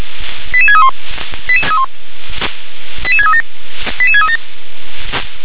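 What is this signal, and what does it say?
Five-tone selective call (ZVEI), heard as quick runs of five tones, tones of about 70 ms. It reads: 88631 88637 88637
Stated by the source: ZVEI five-tone selective call received on SDR.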